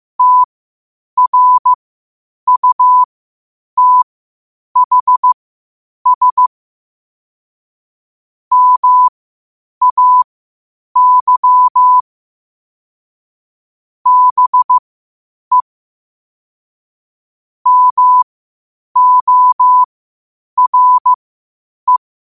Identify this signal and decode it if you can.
Morse code, heard as a keyed tone, about 15 wpm, spaced as for 8 wpm, 1000 Hz. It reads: TRUTHS MAY BE MORE
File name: qotdfiles / 8wpm